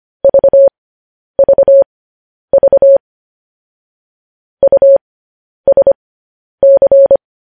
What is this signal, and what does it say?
Morse code, heard as a keyed tone, about 25 wpm, spaced as for 10 wpm, 550 Hz. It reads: VVV USC
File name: podcast / News headlines-1x-25x10